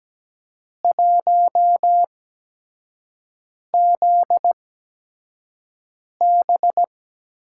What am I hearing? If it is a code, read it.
Morse code, heard as a keyed tone, about 17 wpm, spaced as for 5 wpm, 700 Hz. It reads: 1ZB
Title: USA Call Signs - 5 Characters - 8x Character Spacing - 17wpm